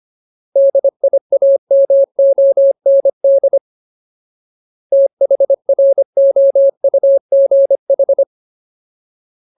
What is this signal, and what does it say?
Morse code, heard as a keyed tone, about 25 wpm, 550 Hz. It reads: DIAMOND THROUGH
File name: Ultimate Word List / 25WPM Ultimate